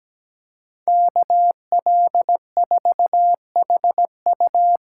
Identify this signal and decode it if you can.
Morse code, heard as a keyed tone, about 20 wpm, 700 Hz. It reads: KL4HU
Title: Speed Racing - USA 2x2 Call Signs 17wpm